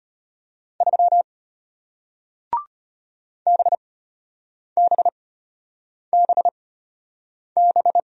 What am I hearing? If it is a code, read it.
Morse code, heard as a keyed tone, about 33 wpm, spaced as for 8 wpm, 700 Hz. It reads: 3 BBBB